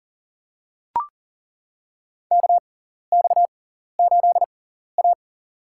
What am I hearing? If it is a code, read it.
Morse code, heard as a keyed tone, about 40 wpm, spaced as for 14 wpm, 700 Hz. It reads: KX8A
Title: USA Call Signs - 6x Character Spacing - 40wpm